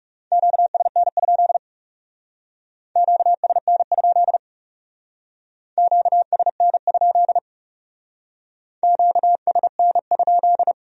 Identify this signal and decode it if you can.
Morse code, heard as a keyed tone, about 37 wpm, 700 Hz. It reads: QSN? QSN? QSN? QSN?
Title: Speed-Racing Q-Codes - 30wpm